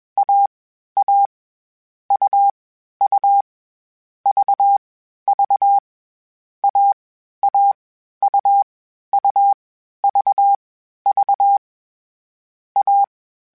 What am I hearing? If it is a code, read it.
Morse code, heard as a keyed tone, about 21 wpm, 800 Hz. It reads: A A U U V V A A U U V V A
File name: CW Code Groups-mp3s / Sets mp3s / AUV AUV4 sp @20